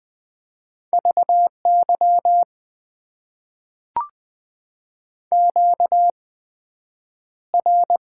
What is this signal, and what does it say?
Morse code, heard as a keyed tone, about 20 wpm, spaced as for 9 wpm, 700 Hz. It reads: VY QR
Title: Top 90 QSO Elements - 8x Character Spacing - 20wpm